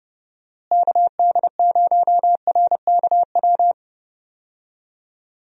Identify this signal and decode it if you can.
Morse code, heard as a keyed tone, about 30 wpm, 700 Hz. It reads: KD0RKW
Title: USA Call Signs - 6 Characters - Repeated 3x - 30wpm